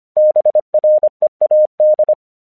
Morse code, heard as a keyed tone, about 24 wpm, 610 Hz. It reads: BREAD
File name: 1 English 25wpm / AQC20